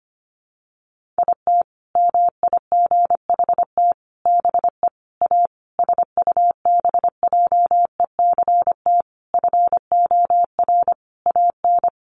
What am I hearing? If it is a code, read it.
Morse code, heard as a keyed tone, about 25 wpm, 700 Hz. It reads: IT MIGHT BE A SUBJECT FOR AN
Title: Sentences from Top 1500 Words 25wpm